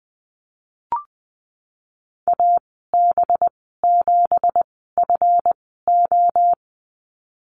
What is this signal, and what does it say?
Morse code, heard as a keyed tone, about 20 wpm, spaced as for 14 wpm, 700 Hz. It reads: AB7FO
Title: 186 Lesson 30g - Letter b - Call Signs 20wpm